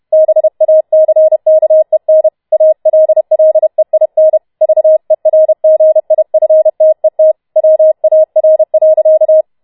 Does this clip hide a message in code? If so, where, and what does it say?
Yes: Morse code from 0.1 to 9.4 s, heard as a keyed tone, about 31 wpm, 630 Hz. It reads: BACKEN ALLEIN VERGIFTET WAR.